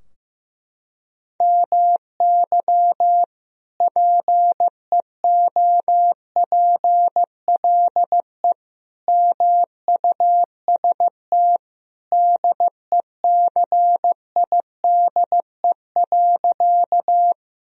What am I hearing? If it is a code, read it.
Morse code, heard as a keyed tone, about 15 wpm, 700 Hz. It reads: MY PEOPLE MUST DECIDE.
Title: Sentences from Top 1000 Words - Review Part 3 - 15wpm